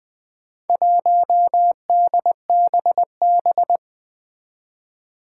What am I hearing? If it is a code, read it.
Morse code, heard as a keyed tone, about 20 wpm, 700 Hz. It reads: 1DBB
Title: Four Challenging Letter-Number Combinations 20wpm